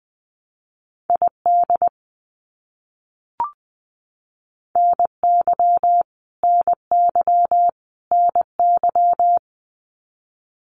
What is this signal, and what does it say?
Morse code, heard as a keyed tone, about 20 wpm, 700 Hz. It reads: ID NY NY NY
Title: US States Repeated 3x 20wpm